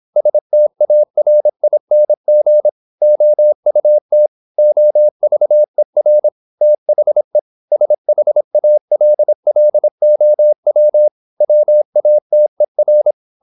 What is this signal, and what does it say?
Morse code, heard as a keyed tone, about 26 wpm, 600 Hz. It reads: STARING OUT OVER THE SHALLOW WATER
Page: Never 26wpm | CW med Gnister